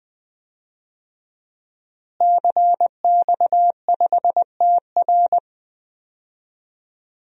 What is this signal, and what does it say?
Morse code, heard as a keyed tone, about 21 wpm, 700 Hz. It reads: CX5TR
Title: DXCC-100 Call Signs - 5 Characters - Repeated 3x - 20wpm